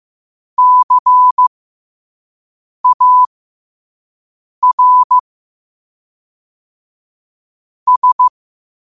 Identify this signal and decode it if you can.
Morse code, heard as a keyed tone, about 15 wpm, spaced as for 5 wpm, 1000 Hz. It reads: CAR S
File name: qotdfiles / 5wpm